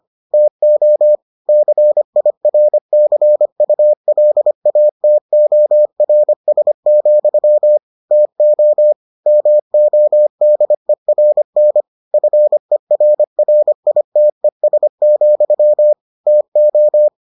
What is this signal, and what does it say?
Morse code, heard as a keyed tone, about 25 wpm, 600 Hz. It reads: TO CIRCULATORS, TO MODERN FERRITES, TO